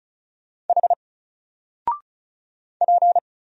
Morse code, heard as a keyed tone, about 35 wpm, 700 Hz. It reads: H P
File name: Single-Letter-Number-35wpm